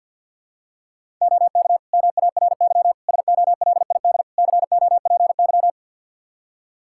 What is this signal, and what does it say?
Morse code, heard as a keyed tone, about 46 wpm, 700 Hz. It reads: OK MARY SOLID COPY
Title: Practice QSO - Exchange 3 - More Info - 50wpm